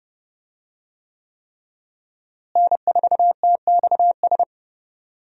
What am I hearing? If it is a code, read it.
Morse code, heard as a keyed tone, about 30 wpm, 700 Hz. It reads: N4TXS
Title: USA Call Signs - 5 Characters - Repeated 3x - 30wpm